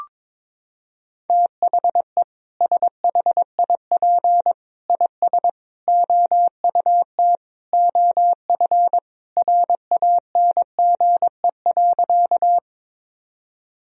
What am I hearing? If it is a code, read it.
Morse code, heard as a keyed tone, about 22 wpm, 700 Hz. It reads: THE SHIP IS OUT OF RANGE.
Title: Sentences from Top 1000 Words - Review Part 4 - 22wpm